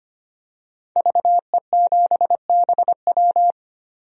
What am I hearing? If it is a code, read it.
Morse code, heard as a keyed tone, about 25 wpm, 700 Hz. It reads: VE7BW
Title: DXCC-100 Call Signs - 5 Characters - 25wpm